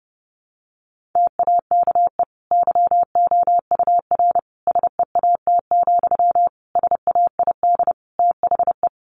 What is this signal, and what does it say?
Morse code, heard as a keyed tone, about 30 wpm, 700 Hz. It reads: TAKE YOUR SEAT, SAID THE
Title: Sentences from Top 1500 Words 30wpm